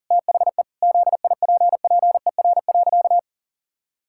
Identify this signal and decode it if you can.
Morse code, heard as a keyed tone, about 40 wpm, 700 Hz. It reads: THE ZIPPER.
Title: Pangram Challenge v2 - 40wpm